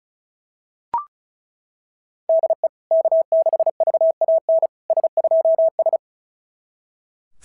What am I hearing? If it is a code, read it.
Morse code, heard as a keyed tone, about 36 wpm, 660 Hz. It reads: DE K6VAN S2S
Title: SOTA QSO Practice 35wpm